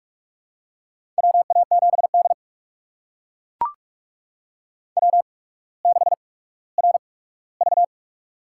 Morse code, heard as a keyed tone, about 44 wpm, spaced as for 19 wpm, 700 Hz. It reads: WA7D W6RV
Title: USA Call Signs - 4 Characters - 8x Character Spacing - 45wpm